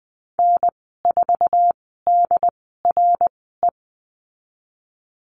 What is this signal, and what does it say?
Morse code, heard as a keyed tone, about 20 wpm, spaced as for 15 wpm, 700 Hz. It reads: N4DRE
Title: 069 Lesson 12g - Letter d - Call Signs 20wpm